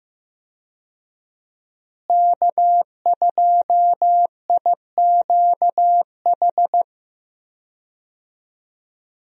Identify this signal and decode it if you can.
Morse code, heard as a keyed tone, about 15 wpm, 700 Hz. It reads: K2IQH
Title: USA Call Signs - 5 Characters - Repeated 3x - 15wpm